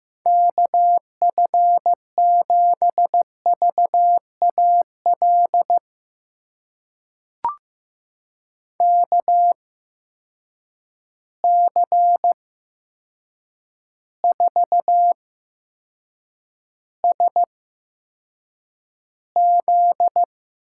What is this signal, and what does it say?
Morse code, heard as a keyed tone, about 15 wpm, spaced as for 7 wpm, 700 Hz. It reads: KF7VAL KC4SZ
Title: USA Call Signs - 6 Characters - 8x Character Spacing - 15wpm